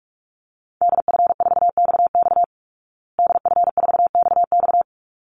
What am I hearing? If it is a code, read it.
Morse code, heard as a keyed tone, about 44 wpm, 700 Hz. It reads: DF4XX DF4XX
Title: Call Signs Repeated 3x - 45wpm